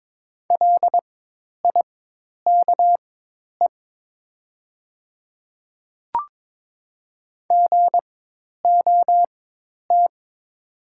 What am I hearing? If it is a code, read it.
Morse code, heard as a keyed tone, about 22 wpm, spaced as for 10 wpm, 690 Hz. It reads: LIKE GOT
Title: 4x Character Spacing - Top 300 Words - 22wpm